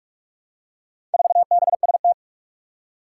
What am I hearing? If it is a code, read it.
Morse code, heard as a keyed tone, about 46 wpm, 700 Hz. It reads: 4BST
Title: Four Challenging Letter-Number Combinations 45wpm